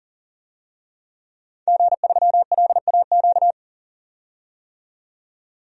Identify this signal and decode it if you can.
Morse code, heard as a keyed tone, about 40 wpm, 700 Hz. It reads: G3LAQ
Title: DXCC-100 Call Signs - 5 Characters - 40wpm